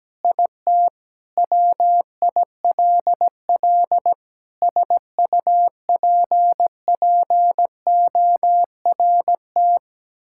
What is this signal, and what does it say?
Morse code, heard as a keyed tone, about 17 wpm, 700 Hz. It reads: IT WILL SUPPORT
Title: Sentences from Top 1000 Words - Review Part 5 - 17wpm